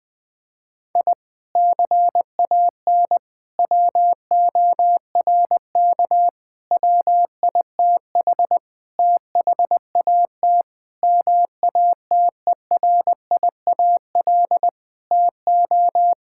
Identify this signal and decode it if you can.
Morse code, heard as a keyed tone, about 20 wpm, 700 Hz. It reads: I CAN WORK WITH THAT MATERIAL TO
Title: Sentences from Top 1000 Words - Review Part 5 - 20wpm